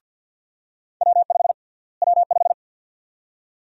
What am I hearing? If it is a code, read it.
Morse code, heard as a keyed tone, about 47 wpm, 700 Hz. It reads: W5 W5
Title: USA Call Sign Prefixes - Repeated 2x - 50wpm